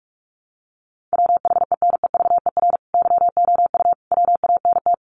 Morse code, heard as a keyed tone, about 44 wpm, 700 Hz. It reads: WHENEVER YOU WANT
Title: Sentences from Top 1200 Words 45wpm